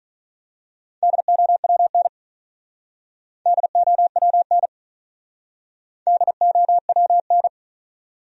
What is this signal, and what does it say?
Morse code, heard as a keyed tone, about 39 wpm, 700 Hz. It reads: DOWN DOWN DOWN
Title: Speed-Racing Top 500 Words 35wpm